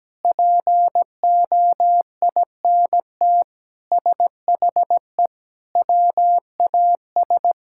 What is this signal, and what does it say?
Morse code, heard as a keyed tone, about 17 wpm, 700 Hz. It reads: POINT SHE WAS
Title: Sentences from Top 1000 Words - Review Part 6 - 17wpm